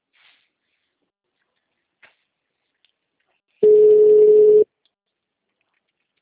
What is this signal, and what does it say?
Ringback tone (Europe, 425 Hz).
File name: out- -2015- - - .51376